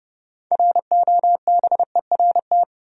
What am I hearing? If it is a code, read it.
Morse code, heard as a keyed tone, about 30 wpm, 700 Hz. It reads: ROBERT
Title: USA Common Names - 30wpm